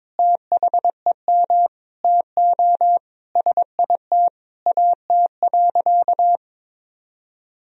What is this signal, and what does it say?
Morse code, heard as a keyed tone, about 22 wpm, 700 Hz. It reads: THEM TO SIT AT.
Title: Sentences from Top 1000 Words - Review Part 2 - 22wpm